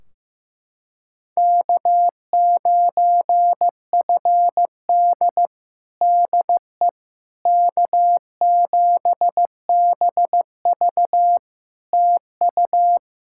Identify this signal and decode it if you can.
Morse code, heard as a keyed tone, about 15 wpm, 700 Hz. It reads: K9FD DE K7BV TU
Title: Practice QSO - Exchange 4 - Ending - 15wpm